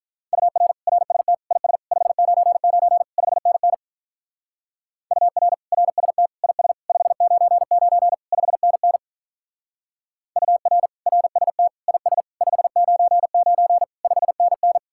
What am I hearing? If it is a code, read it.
Morse code, heard as a keyed tone, about 44 wpm, 700 Hz. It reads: UR RST IS 599 5NN UR RST IS 599 5NN UR RST IS 599 5NN